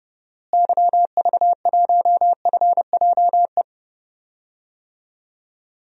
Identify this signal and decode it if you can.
Morse code, heard as a keyed tone, about 30 wpm, 700 Hz. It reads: YV1FJE
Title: DXCC-100 Call Signs - 6 Characters - Repeated 3x - 30wpm